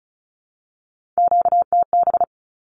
Morse code, heard as a keyed tone, about 34 wpm, 700 Hz. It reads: QTB
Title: Q-Codes Repeated 3x 35wpm